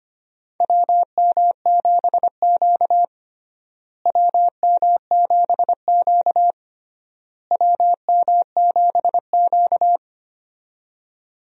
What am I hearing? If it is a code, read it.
Morse code, heard as a keyed tone, about 25 wpm, 700 Hz. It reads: WM7Q WM7Q WM7Q
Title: USA Call Signs - 4 Characters - Repeated 3x - 25wpm